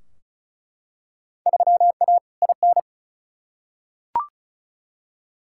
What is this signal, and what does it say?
Morse code, heard as a keyed tone, about 36 wpm, 700 Hz. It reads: 3A IN
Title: Field Day Exchanges - 35wpm